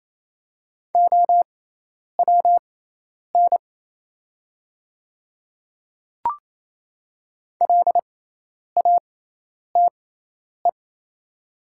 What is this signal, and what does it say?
Morse code, heard as a keyed tone, about 27 wpm, spaced as for 10 wpm, 700 Hz. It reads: OWN LATE
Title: 6x Character Spacing - Top 300 Words - 28wpm